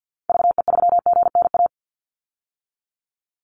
Morse code, heard as a keyed tone, about 50 wpm, 700 Hz. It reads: VE3GNA
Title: Call Signs - 50wpm